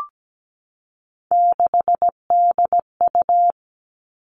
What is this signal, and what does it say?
Morse code, heard as a keyed tone, about 17 wpm, 700 Hz. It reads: 6DU
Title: Three Challenging Letter-Number Combinations 17wpm